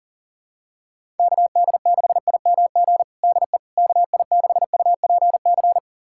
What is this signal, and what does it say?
Morse code, heard as a keyed tone, about 40 wpm, 700 Hz. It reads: KD6IMG DE KI6UPC
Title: USA Callsign DE Callsign 40wpm